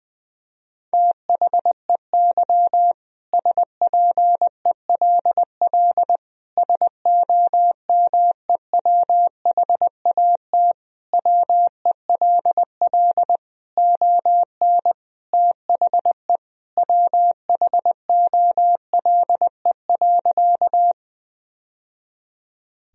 Morse code, heard as a keyed tone, about 19 wpm, 700 Hz. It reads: THEY SPELL SOMEWHAT WELL ON THE WHOLE.
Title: Sentences from Top 1000 Words - Review Part 1 - 20wpm